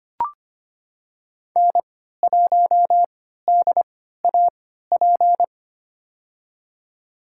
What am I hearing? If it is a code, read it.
Morse code, heard as a keyed tone, about 25 wpm, spaced as for 14 wpm, 700 Hz. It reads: N1DAP
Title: USA Call Signs - 5 Characters - 3x Character Spacing - 25wpm